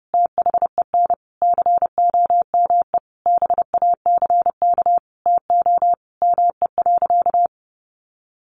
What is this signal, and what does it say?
Morse code, heard as a keyed tone, about 30 wpm, 700 Hz. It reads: THEN COME BACK TO ME.
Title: Sentences from Top 600 Words 30wpm